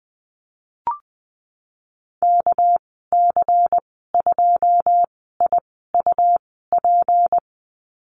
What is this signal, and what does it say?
Morse code, heard as a keyed tone, about 20 wpm, spaced as for 15 wpm, 700 Hz. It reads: KC2IUP
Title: 231 Lesson 37g - Letter x - Call Signs 20wpm